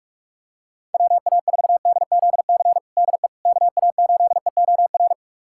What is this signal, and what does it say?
Morse code, heard as a keyed tone, about 46 wpm, 690 Hz. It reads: WA4DZC DE KA8EOR
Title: USA Callsign DE Callsign - 6 Characters - 45wpm